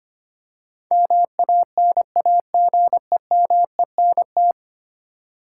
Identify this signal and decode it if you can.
Morse code, heard as a keyed tone, about 25 wpm, 700 Hz. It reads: MANAGEMENT
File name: 10 Letter Words - First 3 Letters Given - 25wpm